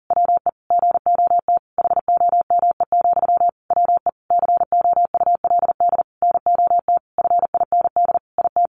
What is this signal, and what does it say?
Morse code, heard as a keyed tone, about 37 wpm, 700 Hz. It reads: WE GOT HOME, WE COULD NOT FIND IT
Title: Sentences from Top 1400 Words 40wpm